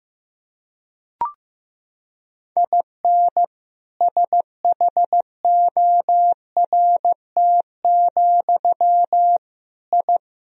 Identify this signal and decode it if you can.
Morse code, heard as a keyed tone, about 16 wpm, 700 Hz. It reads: IN SHORT, I
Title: Sentences from Top 1000 Words - Review Part 2 - 15wpm